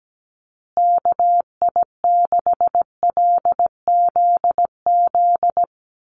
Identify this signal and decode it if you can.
Morse code, heard as a keyed tone, about 17 wpm, 700 Hz. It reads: KI6LZZ
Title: USA Call Signs Repeated 3x 17wpm